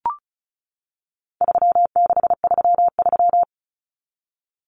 Morse code, heard as a keyed tone, about 34 wpm, 700 Hz. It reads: 3633
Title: Numbers - 4-Digit 35wpm